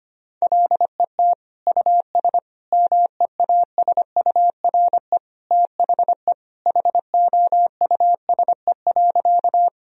Morse code, heard as a keyed tone, about 25 wpm, 700 Hz. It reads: LET US MEASURE THE HOUSE.
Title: Sentences from Top 1000 Words - Review Part 2 - 25wpm